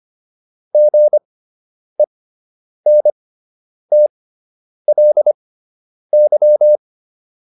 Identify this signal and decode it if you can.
Morse code, heard as a keyed tone, about 25 wpm, spaced as for 9 wpm, 600 Hz. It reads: GENTLY